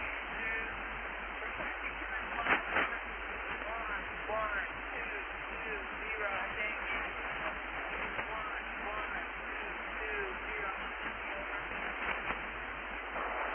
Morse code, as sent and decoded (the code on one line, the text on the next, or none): none